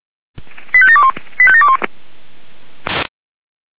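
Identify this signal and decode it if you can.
Five-tone selective call (ZVEI), heard as quick runs of five tones, tones of about 70 ms. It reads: 85721 85721
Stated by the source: Five-tone ZVEI selective call received with an SDR.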